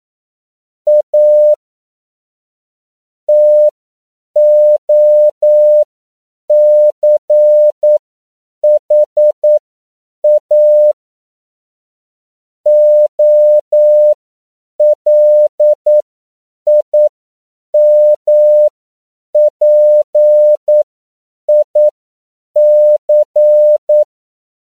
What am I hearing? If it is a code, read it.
Morse code, heard as a keyed tone, about 9 wpm, spaced as for 7 wpm, 600 Hz. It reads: A TOCHA OLIMPIC